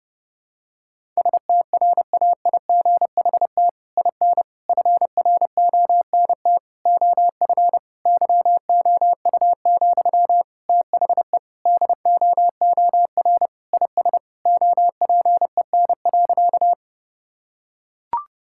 Morse code, heard as a keyed tone, about 30 wpm, 700 Hz. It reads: STRAIGHT IN FRONT OF YOU, THE DOOR IS OPEN.
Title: Sentences from Top 1000 Words - Review Part 6 - 30wpm